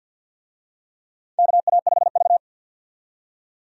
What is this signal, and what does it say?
Morse code, heard as a keyed tone, about 50 wpm, 700 Hz. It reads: KA5V
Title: USA Call Signs - 4 Characters - 50wpm